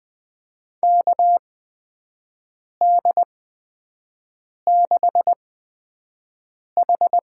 Morse code, heard as a keyed tone, about 20 wpm, spaced as for 5 wpm, 700 Hz. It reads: KD6H
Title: USA Call Signs - 8x Character Spacing - 20wpm